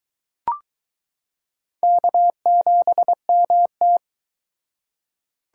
Morse code, heard as a keyed tone, about 21 wpm, 700 Hz. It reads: K7MT